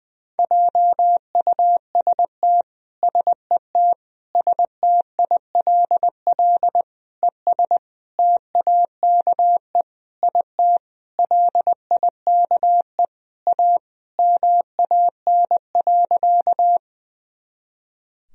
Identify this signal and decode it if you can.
Morse code, heard as a keyed tone, about 21 wpm, 700 Hz. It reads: JUST SET STILL ES TAKE IT LIKE A MAN.
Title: Sentences from Top 300 Words and Abbreviations 20wpm